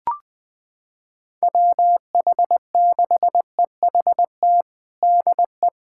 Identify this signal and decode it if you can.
Morse code, heard as a keyed tone, about 20 wpm, 700 Hz. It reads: WH6EHT DE
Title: USA Callsign DE Callsign 20wpm